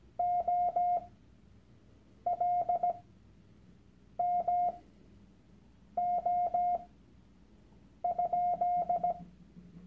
Morse code, heard as a keyed tone, about 17 wpm, spaced as for 6 wpm, 710 Hz. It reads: OLMO?
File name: Workshop Dateien / CW-Online-Kurs / Lektion 001